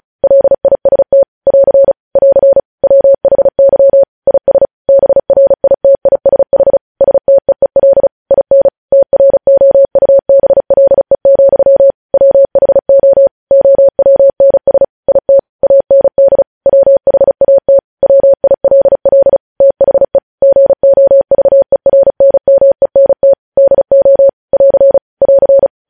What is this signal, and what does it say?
Morse code, heard as a keyed tone, about 36 wpm, 550 Hz. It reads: LIST <AR> <AR> WHY IS BRITISH STEEL IN TROUBLE, WHO OWNS IT AND WHAT WILL THE GOVERNMENT DO <AR> <AR>
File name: podcast / News headlines-1x-35x35